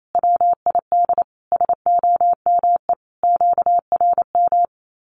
Morse code, heard as a keyed tone, about 28 wpm, 700 Hz. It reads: WID SOME QRM
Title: Common QSO Phrases 28wpm